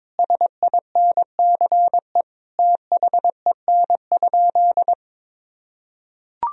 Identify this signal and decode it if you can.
Morse code, heard as a keyed tone, about 22 wpm, 700 Hz. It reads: SINCE THEN?
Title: Sentences from Top 1000 Words - Review Part 1 - 22wpm